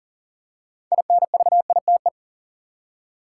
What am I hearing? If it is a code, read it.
Morse code, heard as a keyed tone, about 40 wpm, 700 Hz. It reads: INVITE
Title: Six Letter Words - 40wpm